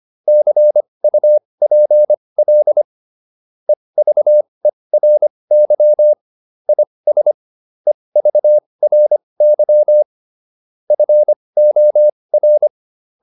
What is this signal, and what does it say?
Morse code, heard as a keyed tone, about 25 wpm, spaced as for 20 wpm, 600 Hz. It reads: CUPL EVERY IS EVRY FOR